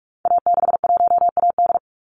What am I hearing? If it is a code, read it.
Morse code, heard as a keyed tone, about 44 wpm, 700 Hz. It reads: A61AD
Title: Call Signs Repeated 2x - 45wpm